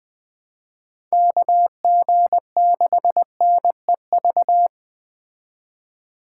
Morse code, heard as a keyed tone, about 20 wpm, 700 Hz. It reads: KG6NEV
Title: Speed Racing - USA 2x3 Call Signs 17wpm